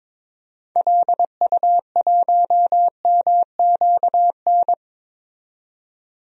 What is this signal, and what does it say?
Morse code, heard as a keyed tone, about 22 wpm, 700 Hz. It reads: LU1MQN
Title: DXCC-100 Call Signs - 6 Characters - 22wpm